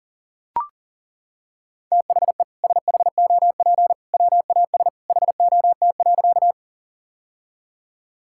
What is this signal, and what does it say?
Morse code, heard as a keyed tone, about 40 wpm, 700 Hz. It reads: THE SHOP WAS HOT.
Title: Sentences from Top 1600 Words 40wpm